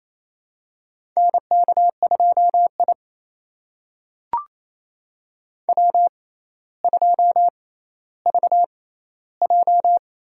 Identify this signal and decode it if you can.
Morse code, heard as a keyed tone, about 28 wpm, spaced as for 14 wpm, 700 Hz. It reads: NK2I W2VJ
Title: USA Call Signs - 4 Characters - 6x Character Spacing - 28wpm